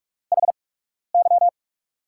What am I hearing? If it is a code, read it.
Morse code, heard as a keyed tone, about 44 wpm, spaced as for 12 wpm, 700 Hz. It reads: HY
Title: USA Call Signs - 5 Characters - 8x Character Spacing - 45wpm